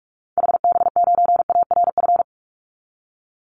Morse code, heard as a keyed tone, about 44 wpm, 700 Hz. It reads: HB9ARF
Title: Call Signs - 45wpm